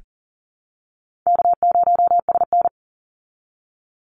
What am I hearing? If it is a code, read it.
Morse code, heard as a keyed tone, about 40 wpm, 700 Hz. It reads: K0SN
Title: Easy Call Signs Repeated 2x - 40wpm